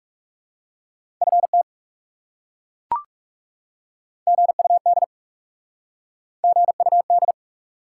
Morse code, heard as a keyed tone, about 40 wpm, 700 Hz. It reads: FT GUD GUD